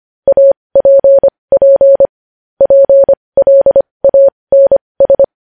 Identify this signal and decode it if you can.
Morse code, heard as a keyed tone, about 25 wpm, spaced as for 20 wpm, 550 Hz. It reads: APP PLANS
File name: podcast / News headlines-1x-25x20